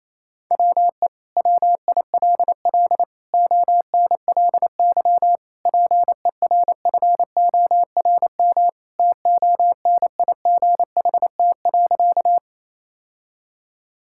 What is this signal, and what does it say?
Morse code, heard as a keyed tone, about 28 wpm, 700 Hz. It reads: WE WILL ONLY PERFORM TONIGHT.
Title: Sentences from Top 1000 Words - Review Part 5 - 28wpm